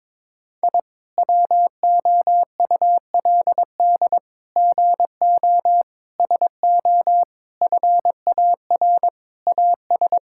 Morse code, heard as a keyed tone, about 22 wpm, 700 Hz. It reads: I WOULD GO SO FAR AS
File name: practice / 22wpm / Sentences with 8-Word Phrases - 22wpm